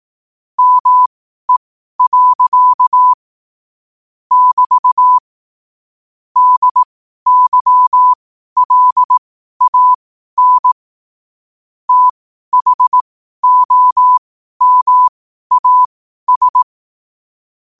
Morse code, heard as a keyed tone, about 18 wpm, spaced as for 12 wpm, 1000 Hz. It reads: ME. <BT> DYLAN THOMAS
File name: qotdfiles / 12wpm